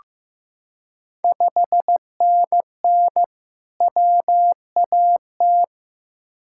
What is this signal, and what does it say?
Morse code, heard as a keyed tone, about 15 wpm, 700 Hz. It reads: 5NN WAT
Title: Ontario QSO Party Exchange 15wpm